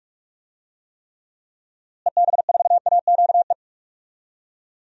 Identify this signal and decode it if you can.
Morse code, heard as a keyed tone, about 46 wpm, 700 Hz. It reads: EB4AQE
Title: DXCC-100 Call Signs - 6 Characters - Repeated 2x - 45wpm